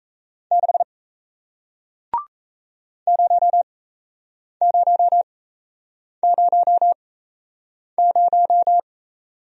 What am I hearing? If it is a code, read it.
Morse code, heard as a keyed tone, about 38 wpm, spaced as for 8 wpm, 700 Hz. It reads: 6 0000